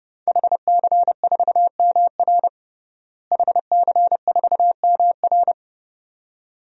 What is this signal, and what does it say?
Morse code, heard as a keyed tone, about 30 wpm, 700 Hz. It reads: HC4MR HC4MR
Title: DXCC-100 Call Signs - 5 Characters - Repeated 2x - 30wpm